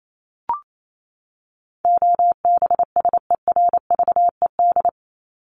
Morse code, heard as a keyed tone, about 28 wpm, 700 Hz. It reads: OBSERVED
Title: ED Suffix - Top 100 Words with Base Words 28wpm